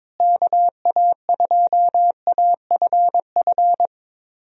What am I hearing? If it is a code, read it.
Morse code, heard as a keyed tone, about 22 wpm, 700 Hz. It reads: KA2AFF
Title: USA Call Signs - 6 Characters - Repeated 3x - 22wpm